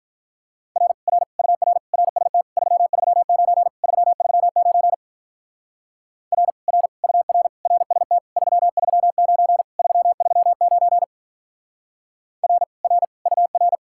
Morse code, heard as a keyed tone, about 43 wpm, 700 Hz. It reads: R R UR RST 339 339 R R UR RST 339 339 R R UR
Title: Speed-Racing Common QSO Phrases Encore - 35wpm